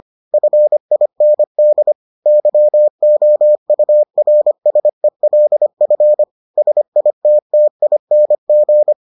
Morse code, heard as a keyed tone, about 25 wpm, 600 Hz. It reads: FIND YOURSELF SITTING